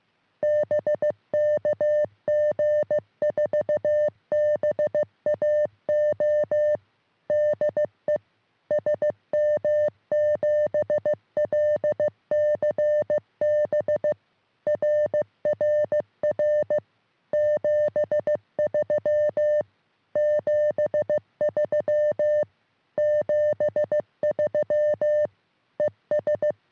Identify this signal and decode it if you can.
Morse code, heard as a keyed tone, about 15 wpm, 590 Hz. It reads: BK G4BAO DE SM7LCB RRR 73 73 73 ES
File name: sm7lcb / g4bao all 23cm